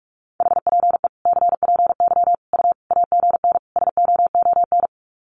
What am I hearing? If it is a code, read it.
Morse code, heard as a keyed tone, about 44 wpm, 700 Hz. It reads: HPE CPY U AGN SOON
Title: Common QSO Phrases 45wpm